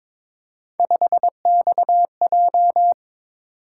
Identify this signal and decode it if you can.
Morse code, heard as a keyed tone, about 21 wpm, 700 Hz. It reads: 5XJ
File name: practice / 22wpm / Three Letter-Number Combinations - 22wpm